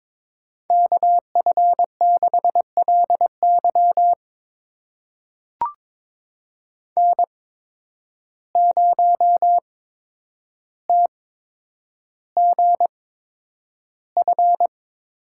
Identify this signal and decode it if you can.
Morse code, heard as a keyed tone, about 22 wpm, spaced as for 9 wpm, 700 Hz. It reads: KF6LY N0TGF
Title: USA Call Signs - 5 Characters - 8x Character Spacing - 22wpm